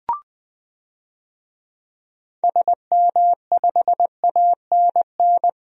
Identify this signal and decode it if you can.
Morse code, heard as a keyed tone, about 20 wpm, 700 Hz. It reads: SM5ANN
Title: DXCC-100 Call Signs - 6 Characters - 20wpm